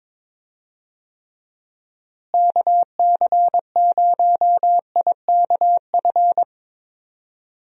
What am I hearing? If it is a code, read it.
Morse code, heard as a keyed tone, about 22 wpm, 700 Hz. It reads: KC0IKF
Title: USA Call Signs - 6 Characters - Repeated 3x - 22wpm